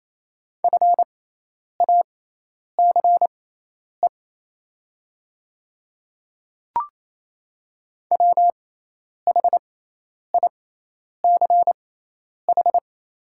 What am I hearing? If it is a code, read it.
Morse code, heard as a keyed tone, about 28 wpm, spaced as for 10 wpm, 700 Hz. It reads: FACE WHICH
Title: 6x Character Spacing - Top 300 Words - 28wpm